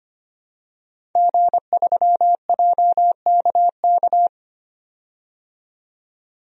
Morse code, heard as a keyed tone, about 25 wpm, 700 Hz. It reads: G3JKK
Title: DXCC-100 Call Signs - 5 Characters - 25wpm